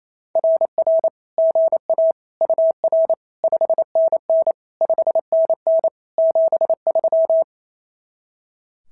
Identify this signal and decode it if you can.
Morse code, heard as a keyed tone, about 28 wpm, 660 Hz. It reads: RR GA UR 5NN 5NN 73
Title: SOTA QSO Practice 28wpm